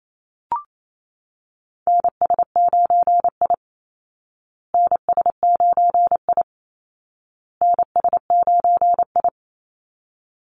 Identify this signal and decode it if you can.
Morse code, heard as a keyed tone, about 28 wpm, 700 Hz. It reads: NS9I NS9I NS9I
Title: Call Signs Repeated 2x - 28wpm